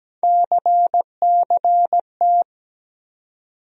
Morse code, heard as a keyed tone, about 17 wpm, 700 Hz. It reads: CCT
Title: USA Call Sign Postfixes - 3 Letters 17wpm